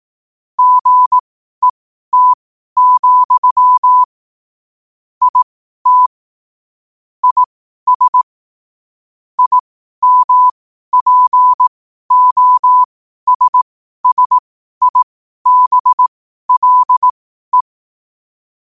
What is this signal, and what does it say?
Morse code, heard as a keyed tone, about 18 wpm, spaced as for 12 wpm, 1000 Hz. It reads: GET, IT IS IMPOSSIBLE
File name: qotdfiles / 12wpm